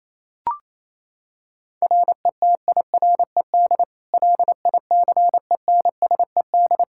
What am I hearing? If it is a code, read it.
Morse code, heard as a keyed tone, about 28 wpm, 700 Hz. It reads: RETIRED LICENSED
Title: Practice QSO - Exchange 3 - More Info - 28wpm